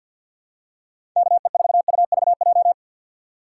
Morse code, heard as a keyed tone, about 53 wpm, 700 Hz. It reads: KE4UVJ
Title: USA Call Signs - 6 Characters - 6x Character Spacing - 50wpm